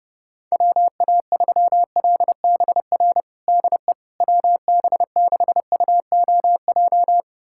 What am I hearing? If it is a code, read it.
Morse code, heard as a keyed tone, about 32 wpm, 700 Hz. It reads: WA3LBR DE WB6UOJ
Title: USA Callsign DE Callsign - 6 Characters - 30wpm